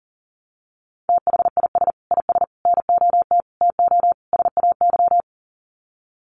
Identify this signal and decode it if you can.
Morse code, heard as a keyed tone, about 40 wpm, 700 Hz. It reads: THIS IS NOT TO SAY
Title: Sets of 5 Words Encore 40wpm